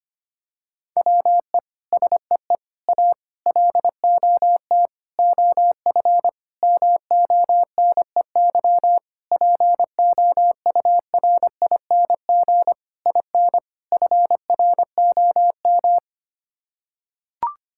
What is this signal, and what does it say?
Morse code, heard as a keyed tone, about 25 wpm, 700 Hz. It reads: WE SEE A LOT OF MONEY POURING IN FROM
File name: Sets of 9 Words - 25wpm